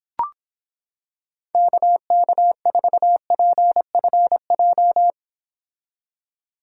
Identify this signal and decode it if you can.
Morse code, heard as a keyed tone, about 26 wpm, 700 Hz. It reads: KK4PFJ